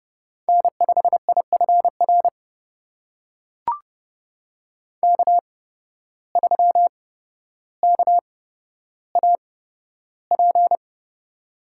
Morse code, heard as a keyed tone, about 31 wpm, spaced as for 13 wpm, 700 Hz. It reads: N5IFR K3KAP
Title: USA Call Signs - 5 Characters - 8x Character Spacing - 30wpm